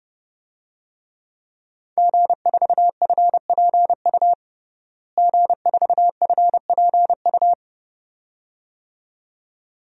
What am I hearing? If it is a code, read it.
Morse code, heard as a keyed tone, about 30 wpm, 700 Hz. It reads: G4FPU G4FPU
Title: DXCC-100 Call Signs - 5 Characters - Repeated 2x - 30wpm